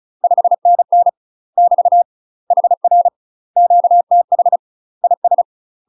Morse code, heard as a keyed tone, about 36 wpm, 700 Hz. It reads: 5NN <BT> HR QTH IS
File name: UK MASTER QSOs TRAINING / 35WPM / UK MASTER QSOs 0036